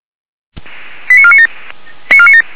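Five-tone selective call (ZVEI), heard as quick runs of five tones, tones of about 70 ms. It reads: 89497
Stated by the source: Five-tone ZVEI selective call received on an SDR, from the feuerwehr-orlamuende.org fire brigade server.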